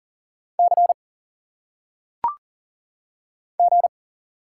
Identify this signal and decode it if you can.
Morse code, heard as a keyed tone, about 40 wpm, 700 Hz. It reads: C G
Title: Single Letters - Phonetic Pronunciation 40wpm